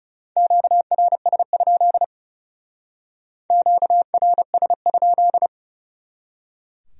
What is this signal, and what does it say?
Morse code, heard as a keyed tone, about 36 wpm, 700 Hz. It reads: QRS? QRS?
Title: Speed-Racing Q-Codes - 30wpm